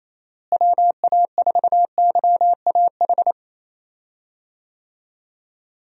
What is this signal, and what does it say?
Morse code, heard as a keyed tone, about 28 wpm, 700 Hz. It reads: WA4YAH
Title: USA Call Signs - 6 Characters - Repeated 2x - 28wpm